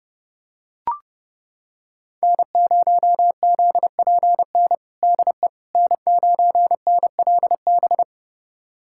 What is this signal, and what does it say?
Morse code, heard as a keyed tone, about 30 wpm, 700 Hz. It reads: N0ZPN DE N9NLB
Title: USA Callsign DE Callsign 30wpm